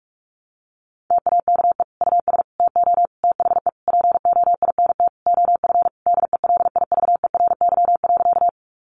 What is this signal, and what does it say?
Morse code, heard as a keyed tone, about 44 wpm, 700 Hz. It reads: TAKE US TO THE POINT OF DELIVERY.
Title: Sentences from Top 1500 Words 45wpm